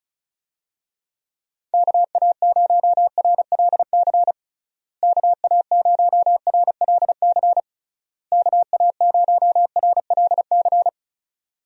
Morse code, heard as a keyed tone, about 36 wpm, 700 Hz. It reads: KA0RLC KA0RLC KA0RLC
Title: USA Call Signs - 6 Characters - Repeated 3x - 35wpm